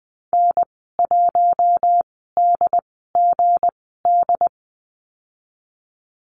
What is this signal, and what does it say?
Morse code, heard as a keyed tone, about 20 wpm, spaced as for 15 wpm, 700 Hz. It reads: N1DGD
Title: 161 Lesson 26g - Letter g - Call Signs 20wpm